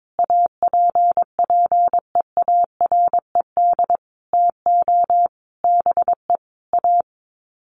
Morse code, heard as a keyed tone, about 22 wpm, 700 Hz. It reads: APPEARED TO BE A
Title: Sets of 5 Words Encore2 22wpm